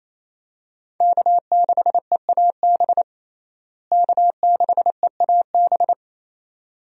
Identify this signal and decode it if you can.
Morse code, heard as a keyed tone, about 28 wpm, 700 Hz. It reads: K6EAB K6EAB
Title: USA Call Signs - 5 Characters - Repeated 2x - 28wpm